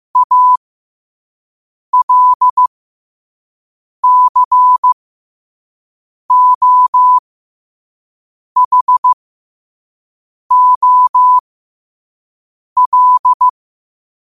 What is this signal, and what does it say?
Morse code, heard as a keyed tone, about 15 wpm, spaced as for 5 wpm, 1000 Hz. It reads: ALCOHOL
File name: 5wpm